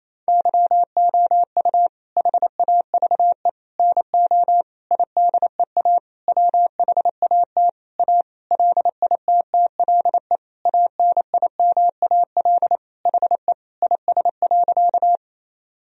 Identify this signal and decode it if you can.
Morse code, heard as a keyed tone, about 28 wpm, 700 Hz. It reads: YOU HAVE NO IDEA WHAT A LITTLE ANIMAL HE IS.
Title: Sentences from Top 1000 Words - Review Part 1 - 28wpm